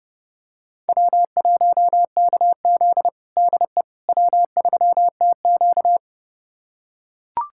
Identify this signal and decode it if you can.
Morse code, heard as a keyed tone, about 29 wpm, 700 Hz. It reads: W1KZ DE W3TQ
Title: USA Callsign DE Callsign - 4 Characters - 30wpm